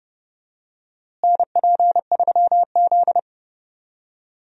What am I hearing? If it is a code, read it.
Morse code, heard as a keyed tone, about 30 wpm, 700 Hz. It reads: NP3Z
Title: USA Call Signs - 4 Characters - Repeated 3x - 30wpm